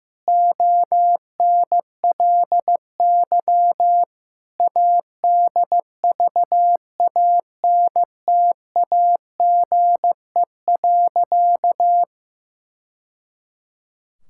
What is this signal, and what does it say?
Morse code, heard as a keyed tone, about 15 wpm, 700 Hz. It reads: ONLY ADVANTAGE.
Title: Sentences from Top 1000 Words - Review Part 6 - 15wpm